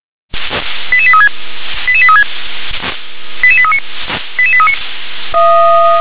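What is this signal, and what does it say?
Five-tone selective call (ZVEI), heard as quick runs of five tones, tones of about 70 ms. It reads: 88836 88836 88839 88839
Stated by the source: ZVEI five-tone call received on an SDR.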